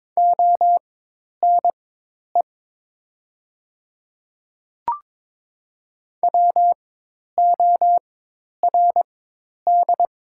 Morse code, heard as a keyed tone, about 22 wpm, spaced as for 10 wpm, 690 Hz. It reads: ONE WORD
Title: 4x Character Spacing - Top 300 Words - 22wpm